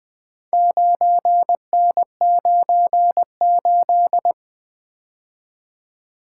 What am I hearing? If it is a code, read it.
Morse code, heard as a keyed tone, about 20 wpm, 700 Hz. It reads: 9N98
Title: Four Letter-Number Combinations - 20wpm